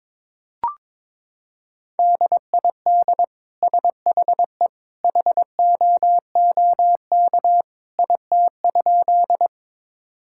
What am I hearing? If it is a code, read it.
Morse code, heard as a keyed tone, about 22 wpm, 700 Hz. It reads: DID SHE HOOK IT?
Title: Sentences from Top 1600 Words 22wpm